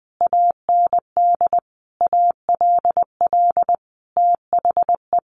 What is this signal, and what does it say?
Morse code, heard as a keyed tone, about 20 wpm, 700 Hz. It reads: AND ALL THE
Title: A Sound Of Thunder 20wpm